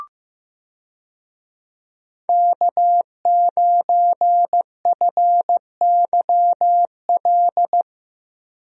Morse code, heard as a keyed tone, about 15 wpm, 700 Hz. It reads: K9FYL
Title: USA Call Signs - 5 Characters - Repeated 2x - 15wpm